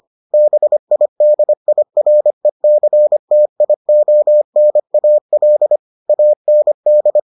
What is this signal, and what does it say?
Morse code, heard as a keyed tone, about 25 wpm, 600 Hz. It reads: BIDIRECTIONAL AND